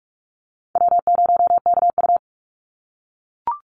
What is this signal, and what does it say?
Morse code, heard as a keyed tone, about 44 wpm, 700 Hz. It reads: W0KU
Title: Easy Call Signs - 45wpm